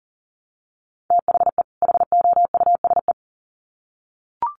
Morse code, heard as a keyed tone, about 40 wpm, 700 Hz. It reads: THE HOUSE
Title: Sets of 2 Words 40wpm